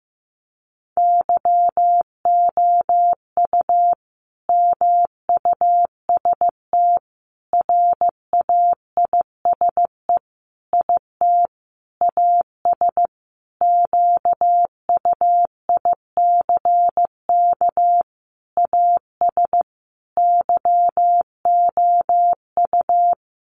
Morse code, heard as a keyed tone, about 15 wpm, 700 Hz. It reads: YOU MUST RAISE IT AS QUICK AS YOU
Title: Sentences from Top 1400 Words 15wpm